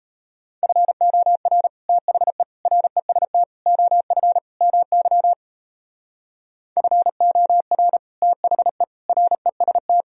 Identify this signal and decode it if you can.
Morse code, heard as a keyed tone, about 38 wpm, 700 Hz. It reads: FOR THE REST OF MY FOR THE REST